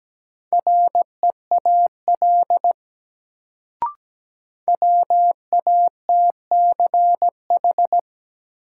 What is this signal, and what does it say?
Morse code, heard as a keyed tone, about 17 wpm, 700 Hz. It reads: REAL WATCH
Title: Sentences from Top 300 Words - New Word Drill - 17wpm